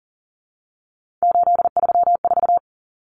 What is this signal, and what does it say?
Morse code, heard as a keyed tone, about 40 wpm, 700 Hz. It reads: 834
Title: Common Cut Numbers - 3-Digit 40wpm